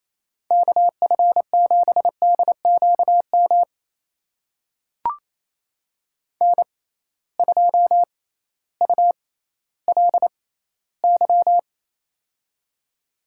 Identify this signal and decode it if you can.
Morse code, heard as a keyed tone, about 28 wpm, spaced as for 15 wpm, 700 Hz. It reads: KF7DQM N2ULY
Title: USA Call Signs - 6x Character Spacing - 28wpm